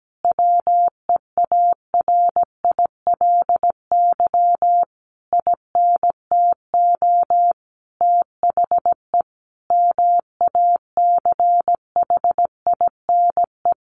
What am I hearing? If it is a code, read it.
Morse code, heard as a keyed tone, about 17 wpm, 700 Hz. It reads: WEARILY INTO THE MACHINE
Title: A Sound Of Thunder 17wpm